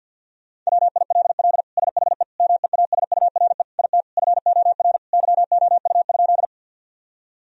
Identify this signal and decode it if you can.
Morse code, heard as a keyed tone, about 53 wpm, 700 Hz. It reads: WILL SHE MEASURE IT FOR YOU?
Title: Sentences from Top 1000 Words - Review Part 2 - 50wpm